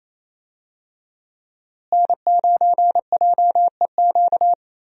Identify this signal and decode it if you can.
Morse code, heard as a keyed tone, about 27 wpm, 700 Hz. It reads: N9JEQ
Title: USA Call Signs - 5 Characters - Repeated 3x - 28wpm